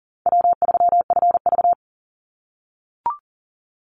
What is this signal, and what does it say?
Morse code, heard as a keyed tone, about 39 wpm, 700 Hz. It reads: W3FV
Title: Easy Call Signs - 40wpm